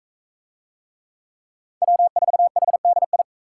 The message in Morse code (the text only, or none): W4HDI